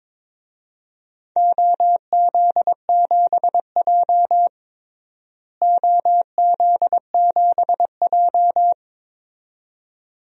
Morse code, heard as a keyed tone, about 22 wpm, 700 Hz. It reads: OZ7J OZ7J
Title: DXCC-100 Call Signs - 4 Characters - Repeated 2x - 22wpm